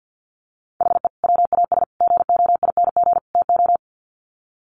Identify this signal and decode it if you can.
Morse code, heard as a keyed tone, about 50 wpm, 700 Hz. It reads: HE WAS GOING TO
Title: Sets of 4 Words 50wpm